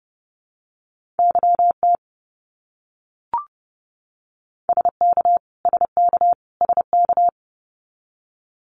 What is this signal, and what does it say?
Morse code, heard as a keyed tone, about 30 wpm, 700 Hz. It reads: YT SK SK SK
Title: Canadian Provinces and Territories - Repeated 3x - 30wpm